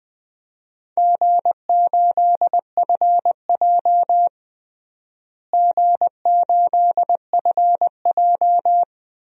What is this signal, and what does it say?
Morse code, heard as a keyed tone, about 21 wpm, 700 Hz. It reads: G8FJ G8FJ
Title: DXCC-100 Call Signs - 4 Characters - Repeated 3x - 20wpm